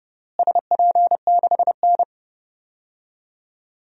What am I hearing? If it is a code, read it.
Morse code, heard as a keyed tone, about 30 wpm, 700 Hz. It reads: SP6N
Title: DXCC-100 Call Signs - 4 Characters - Repeated 3x - 30wpm